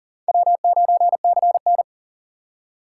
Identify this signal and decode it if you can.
Morse code, heard as a keyed tone, about 40 wpm, 700 Hz. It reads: W9CN
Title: USA Call Signs - 4 Characters - 2x Character Spacing - 40wpm